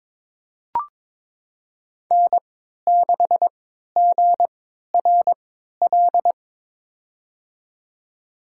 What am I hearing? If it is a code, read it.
Morse code, heard as a keyed tone, about 23 wpm, spaced as for 13 wpm, 700 Hz. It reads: N6GRL
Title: USA Call Signs - 3x Character Spacing - 22wpm